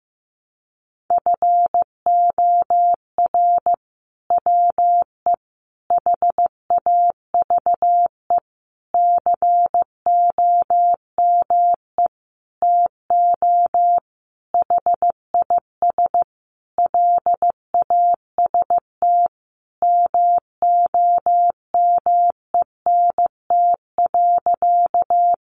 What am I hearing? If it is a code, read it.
Morse code, heard as a keyed tone, about 15 wpm, 700 Hz. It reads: FOR WE HAVE COME TO HIS LAST MOMENT.
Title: Sentences from Top 1000 Words 15wpm